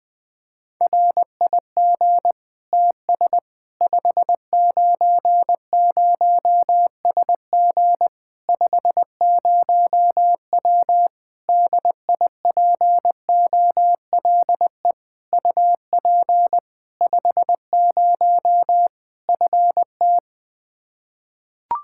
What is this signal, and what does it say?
Morse code, heard as a keyed tone, about 21 wpm, 700 Hz. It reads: RIG TS 590SG 50W DIPOLE UP 50 FT
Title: Practice QSO - Exchange 2 - Info - 20wpm